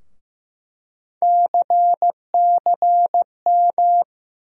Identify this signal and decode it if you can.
Morse code, heard as a keyed tone, about 15 wpm, 700 Hz. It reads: CCM
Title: USA Call Sign Postfixes - 3 Letters 15wpm